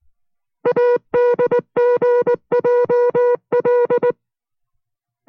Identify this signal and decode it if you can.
Morse code, heard as a keyed tone, about 19 wpm, 490 Hz. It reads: ADGJL